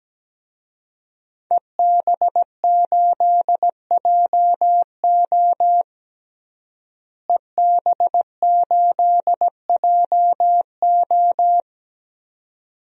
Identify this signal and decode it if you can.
Morse code, heard as a keyed tone, about 17 wpm, 700 Hz. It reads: EB8JO EB8JO
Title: DXCC-100 Call Signs - 5 Characters - Repeated 2x - 17wpm